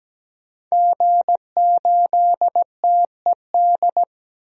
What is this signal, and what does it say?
Morse code, heard as a keyed tone, about 17 wpm, 700 Hz. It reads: G8TED
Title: DXCC-100 Call Signs - 5 Characters - 17wpm